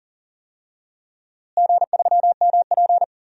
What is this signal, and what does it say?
Morse code, heard as a keyed tone, about 40 wpm, 700 Hz. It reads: G3MP
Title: DXCC-100 Call Signs - 4 Characters - Repeated 2x - 40wpm